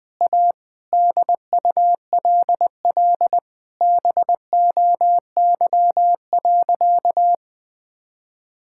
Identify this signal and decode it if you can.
Morse code, heard as a keyed tone, about 20 wpm, 700 Hz. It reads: A DULL BOY.
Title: Sentences with 8-Word Phrases - Spoken Beginning - 20wpm